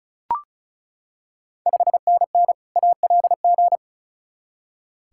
Morse code, heard as a keyed tone, about 36 wpm, 700 Hz. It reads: 5NN ALG
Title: Ontario QSO Party Exchange 35wpm